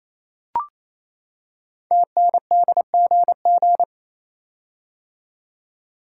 Morse code, heard as a keyed tone, about 27 wpm, 700 Hz. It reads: TNDGG
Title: Code Groups - 5 Letters - 28wpm